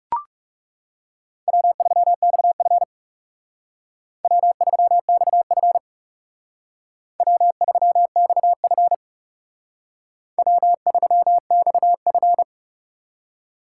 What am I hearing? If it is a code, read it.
Morse code, heard as a keyed tone, about 39 wpm, 700 Hz. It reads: W3XF W3XF W3XF W3XF